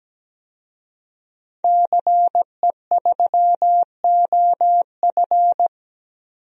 Morse code, heard as a keyed tone, about 17 wpm, 700 Hz. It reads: CE3OF
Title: DXCC-100 Call Signs - 5 Characters - Repeated 2x - 17wpm